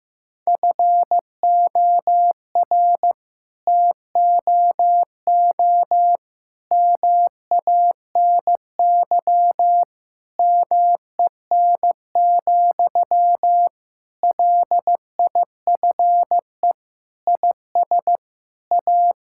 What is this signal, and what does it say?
Morse code, heard as a keyed tone, about 15 wpm, 700 Hz. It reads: FOR TOO MANY MEN, LIFE IS A
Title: Sentences from Top 1000 Words - Review Part 6 - 15wpm